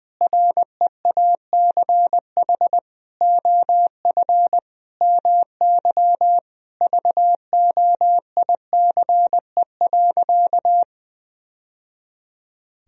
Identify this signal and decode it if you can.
Morse code, heard as a keyed tone, about 20 wpm, 700 Hz. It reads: REACH OF MY VOICE.
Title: Sentences from Top 1000 Words - Review Part 4 - 20wpm